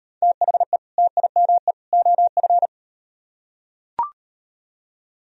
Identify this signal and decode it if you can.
Morse code, heard as a keyed tone, about 38 wpm, 700 Hz. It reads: THE TIME OF